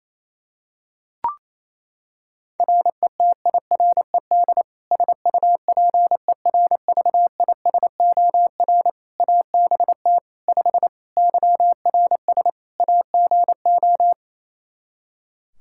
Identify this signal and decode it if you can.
Morse code, heard as a keyed tone, about 28 wpm, 700 Hz. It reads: RETIRED SUPERVISOR ABT 5 YRS AGO
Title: Practice QSO - Exchange 3 - More Info - 28wpm